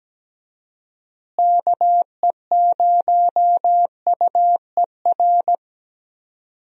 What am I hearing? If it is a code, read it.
Morse code, heard as a keyed tone, about 17 wpm, 700 Hz. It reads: KE0UER
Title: USA Call Signs - 6 Characters - Repeated 3x - 17wpm